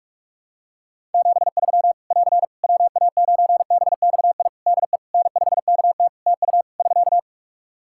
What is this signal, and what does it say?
Morse code, heard as a keyed tone, about 48 wpm, 700 Hz. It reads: 73 <AR> WA9BXI DE N5KT TU <SK>
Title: Practice QSO - Exchange 4 - Ending - 45wpm